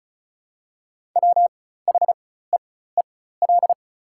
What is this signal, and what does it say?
Morse code, heard as a keyed tone, about 34 wpm, spaced as for 16 wpm, 690 Hz. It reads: WHEEL
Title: 4x Character Spacing - Top 500 Words - 35wpm